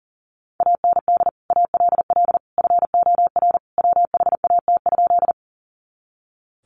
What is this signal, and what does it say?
Morse code, heard as a keyed tone, about 37 wpm, 700 Hz. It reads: AND ALL FOR WHAT?
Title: Sentences from Top 100 Words 40wpm